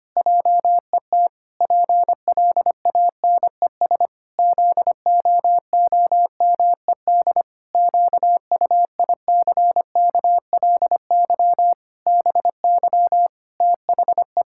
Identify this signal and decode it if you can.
Morse code, heard as a keyed tone, about 25 wpm, 700 Hz. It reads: JET PLANES ZOOMED QUICKLY BY THE
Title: Pangram Challenge v2 - 25wpm